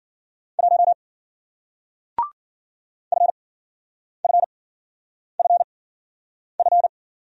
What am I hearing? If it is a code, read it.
Morse code, heard as a keyed tone, about 55 wpm, spaced as for 9 wpm, 700 Hz. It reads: 1 FFFF